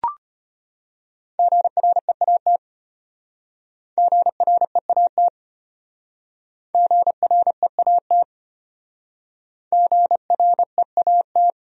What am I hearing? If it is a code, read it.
Morse code, heard as a keyed tone, about 34 wpm, 700 Hz. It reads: GREAT GREAT GREAT GREAT